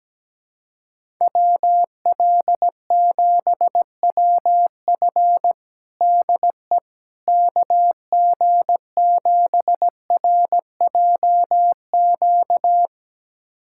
Morse code, heard as a keyed tone, about 17 wpm, 700 Hz. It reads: WL7WF DE KG7RJQ
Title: USA Callsign DE Callsign 17wpm